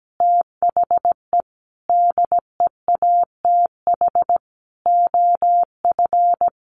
Morse code, heard as a keyed tone, about 17 wpm, 700 Hz. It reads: THE DEATH OF
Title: Sentences from Top 1400 Words 17wpm